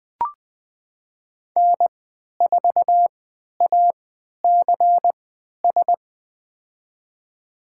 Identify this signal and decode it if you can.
Morse code, heard as a keyed tone, about 20 wpm, spaced as for 11 wpm, 700 Hz. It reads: N4ACS
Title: USA Call Signs - 5 Characters - 3x Character Spacing - 20wpm